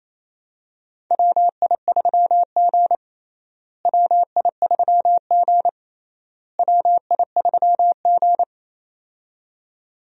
Morse code, heard as a keyed tone, about 28 wpm, 700 Hz. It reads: WI3G WI3G WI3G
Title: USA Call Signs - 4 Characters - Repeated 3x - 28wpm